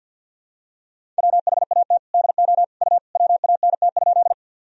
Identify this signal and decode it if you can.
Morse code, heard as a keyed tone, about 52 wpm, 700 Hz. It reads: WHAT DO U WANT?
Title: Sentences from Top 200 Words and Abbreviations 50wpm